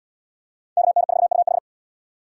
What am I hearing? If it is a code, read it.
Morse code, heard as a keyed tone, about 65 wpm, 700 Hz. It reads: KN4UH